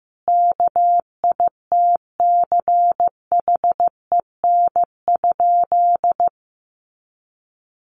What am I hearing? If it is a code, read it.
Morse code, heard as a keyed tone, about 15 wpm, 700 Hz. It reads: KITCHEN?
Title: Sentences from Top 1400 Words 15wpm